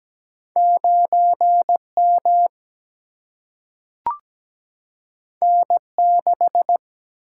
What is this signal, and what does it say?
Morse code, heard as a keyed tone, about 17 wpm, 700 Hz. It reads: 9 M N 6
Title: Single-Letter Single-Number 17wpm